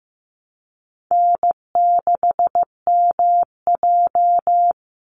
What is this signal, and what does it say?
Morse code, heard as a keyed tone, about 15 wpm, 700 Hz. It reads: N6MJ
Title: Easy Call Signs Repeated 3x - 15wpm